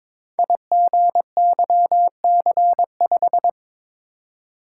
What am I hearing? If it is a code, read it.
Morse code, heard as a keyed tone, about 22 wpm, 690 Hz. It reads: IGYC5
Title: Code Groups - 5 Letter:Numbers - 22wpm